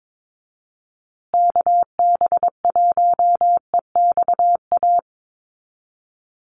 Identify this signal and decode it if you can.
Morse code, heard as a keyed tone, about 22 wpm, 700 Hz. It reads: KB1EXA
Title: USA Call Signs Repeated 3x 22wpm